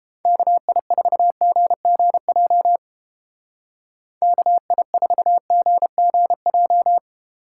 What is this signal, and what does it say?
Morse code, heard as a keyed tone, about 29 wpm, 700 Hz. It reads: KI4GGJ KI4GGJ